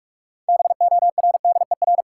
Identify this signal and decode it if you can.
Morse code, heard as a keyed tone, about 46 wpm, 700 Hz. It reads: BORDER
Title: Six Letter Words - 45wpm